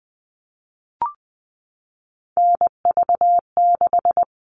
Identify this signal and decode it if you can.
Morse code, heard as a keyed tone, about 20 wpm, 700 Hz. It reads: NV6
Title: All Cut Numbers - 3-Digit 20wpm